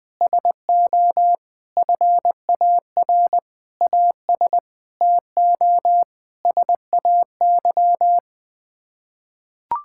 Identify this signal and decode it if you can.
Morse code, heard as a keyed tone, about 21 wpm, 700 Hz. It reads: SO FAR AS TO SAY
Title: Sets of 8 Words - 20wpm